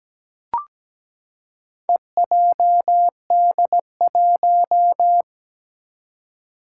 Morse code, heard as a keyed tone, about 17 wpm, 700 Hz. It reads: EJD1
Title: Four Challenging Letter-Number Combinations 17wpm